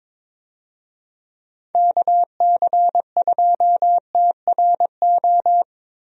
Morse code, heard as a keyed tone, about 22 wpm, 700 Hz. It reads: KC2TRO
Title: USA Call Signs - 6 Characters - 22wpm